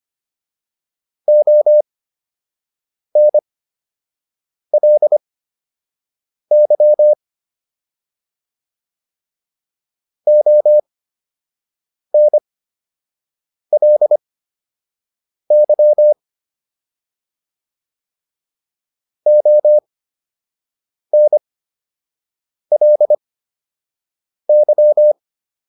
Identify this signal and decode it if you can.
Morse code, heard as a keyed tone, about 25 wpm, spaced as for 6 wpm, 600 Hz. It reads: ONLY ONLY ONLY